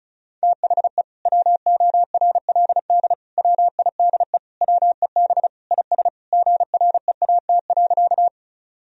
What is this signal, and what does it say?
Morse code, heard as a keyed tone, about 34 wpm, 700 Hz. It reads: THE WORLD WIDE WEB IS GREAT.
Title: Sentences from Top 1000 Words - Review Part 4 - 35wpm